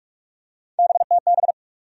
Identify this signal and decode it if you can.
Morse code, heard as a keyed tone, about 42 wpm, 700 Hz. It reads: BTB
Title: USA Call Sign Postfixes - 3 Letters 45wpm